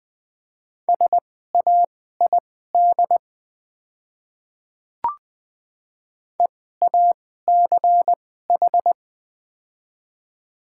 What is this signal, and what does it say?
Morse code, heard as a keyed tone, about 20 wpm, spaced as for 14 wpm, 700 Hz. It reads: SAID EACH
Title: 2x Character Spacing - Top 200 Words - 20wpm